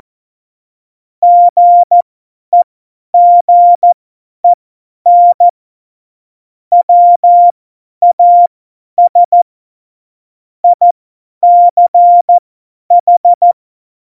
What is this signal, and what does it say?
Morse code, heard as a keyed tone, about 14 wpm, spaced as for 10 wpm, 700 Hz. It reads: GEGEN WAS ICH
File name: Morsekurs / Witze / Witz 05. Dezember 2017, 50 F70